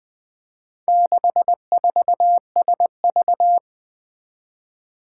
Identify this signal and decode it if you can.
Morse code, heard as a keyed tone, about 20 wpm, 700 Hz. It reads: 64SV
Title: Four Challenging Letter-Number Combinations 20wpm